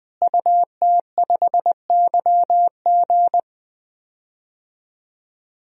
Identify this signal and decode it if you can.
Morse code, heard as a keyed tone, about 20 wpm, 700 Hz. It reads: UT5YG
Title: DXCC-100 Call Signs - 5 Characters - Repeated 3x - 20wpm